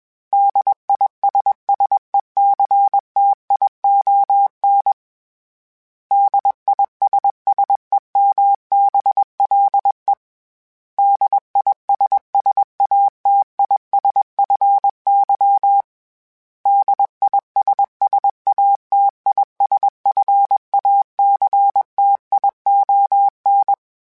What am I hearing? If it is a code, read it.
Morse code, heard as a keyed tone, about 21 wpm, 800 Hz. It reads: DISSECTION DISSEMBLE DISSATISFY DISSATISFACTION
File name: Special INGs / Special INGs-mp3s / Suffixes @20